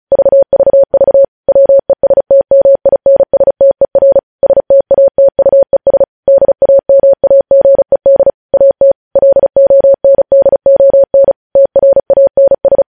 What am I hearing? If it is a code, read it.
Morse code, heard as a keyed tone, about 36 wpm, 550 Hz. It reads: VVV WESTMINSTER STATUES DAMAGED AT LONDON TRANS
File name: podcast / News headlines-1x-35x35